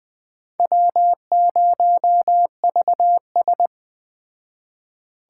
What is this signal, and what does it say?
Morse code, heard as a keyed tone, about 20 wpm, 700 Hz. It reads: W0VS